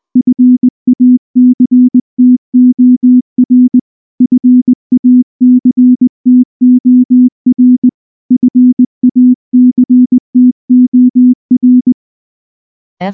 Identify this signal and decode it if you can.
Morse code, heard as a keyed tone, about 19 wpm, 250 Hz. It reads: FACTOR FACTOR FACTOR
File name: morse / morse-training-20WPM-2